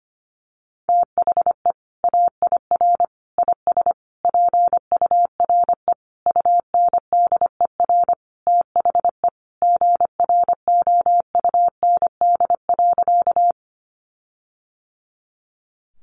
Morse code, heard as a keyed tone, about 25 wpm, 700 Hz. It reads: THE AIR IS PURE UNDER THE GROUND.
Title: Sentences from Top 1500 Words 25wpm